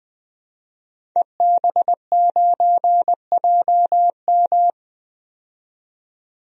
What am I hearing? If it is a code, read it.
Morse code, heard as a keyed tone, about 20 wpm, 700 Hz. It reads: EB9JM
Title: DXCC-100 Call Signs - 5 Characters - 20wpm